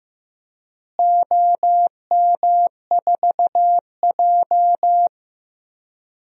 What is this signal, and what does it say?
Morse code, heard as a keyed tone, about 15 wpm, 700 Hz. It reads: OM4J
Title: DXCC-100 Call Signs - 4 Characters - Repeated 3x - 15wpm